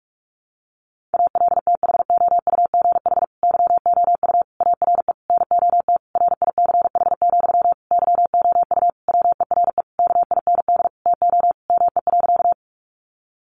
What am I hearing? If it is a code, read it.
Morse code, heard as a keyed tone, about 44 wpm, 700 Hz. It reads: ALTHOUGH YOU ARE NOT RICH, YOU WERE KIND TO ME.
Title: Sentences from Top 1100 Words 45wpm